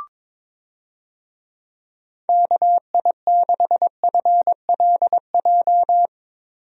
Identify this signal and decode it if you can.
Morse code, heard as a keyed tone, about 22 wpm, 700 Hz. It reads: KI6FLJ
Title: USA Call Signs - 6 Characters - 22wpm